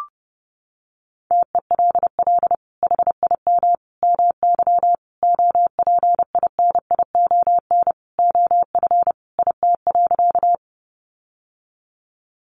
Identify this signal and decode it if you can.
Morse code, heard as a keyed tone, about 30 wpm, 700 Hz. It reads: TELL HIM MY OPINION OF IT.
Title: Sentences from Top 1200 Words 30wpm